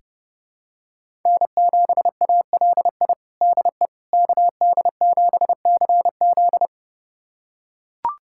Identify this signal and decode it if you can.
Morse code, heard as a keyed tone, about 30 wpm, 700 Hz. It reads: N7ALI DE KD7CZ
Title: USA Callsign DE Callsign - 5 Characters - 30wpm